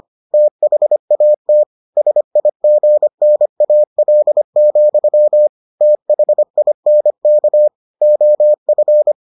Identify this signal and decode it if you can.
Morse code, heard as a keyed tone, about 25 wpm, 600 Hz. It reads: THAT SIGNAL, THINK OF